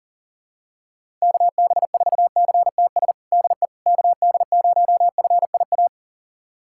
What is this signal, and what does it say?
Morse code, heard as a keyed tone, about 40 wpm, 700 Hz. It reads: KB4CTS DE KD0FIA
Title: USA Callsign DE Callsign 40wpm